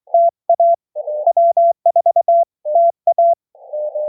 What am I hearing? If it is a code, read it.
Morse code, heard as a keyed tone, about 23 wpm, 690 Hz. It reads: TA W4TA